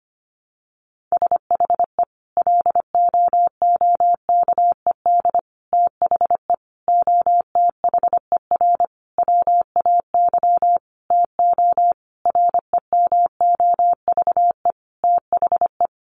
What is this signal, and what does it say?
Morse code, heard as a keyed tone, about 25 wpm, 700 Hz. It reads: SHE LOOKED THE OTHER WAY TO REMOVE THE
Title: Sentences from Top 800 Words 25wpm